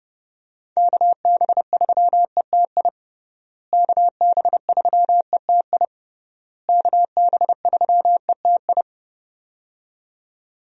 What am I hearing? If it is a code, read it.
Morse code, heard as a keyed tone, about 32 wpm, 700 Hz. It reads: KB3ETI KB3ETI KB3ETI
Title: USA Call Signs - 6 Characters - Repeated 3x - 30wpm